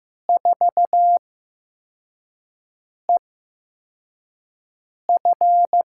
Morse code, heard as a keyed tone, about 15 wpm, spaced as for 4 wpm, 700 Hz. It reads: 4EF
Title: USA Call Signs - 4 Characters - 8x Character Spacing - 15wpm